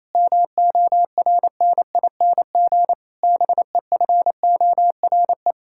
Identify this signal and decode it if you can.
Morse code, heard as a keyed tone, about 28 wpm, 700 Hz. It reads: MORNING BEFORE
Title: Pangram Challenge v2 - 28wpm